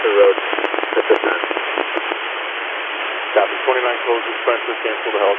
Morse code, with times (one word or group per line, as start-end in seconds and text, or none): none